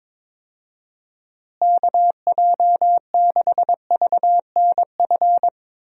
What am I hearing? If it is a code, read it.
Morse code, heard as a keyed tone, about 22 wpm, 700 Hz. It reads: KJ6VNF
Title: USA Call Signs - 6 Characters - Repeated 3x - 22wpm